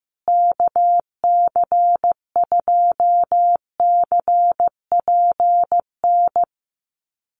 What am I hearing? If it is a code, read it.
Morse code, heard as a keyed tone, about 15 wpm, 700 Hz. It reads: KC2CPN
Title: USA Call Signs Repeated 3x 15wpm